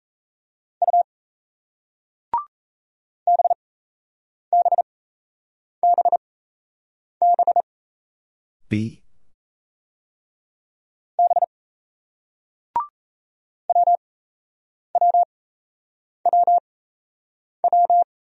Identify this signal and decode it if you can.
Morse code, heard as a keyed tone, about 35 wpm, spaced as for 8 wpm, 700 Hz. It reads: U BBBB B WWWW
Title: Speed-Racing Single Letters 28wpm